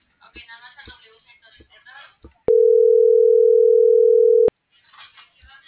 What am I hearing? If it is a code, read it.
Ringback tone (North America), 440 + 480 Hz.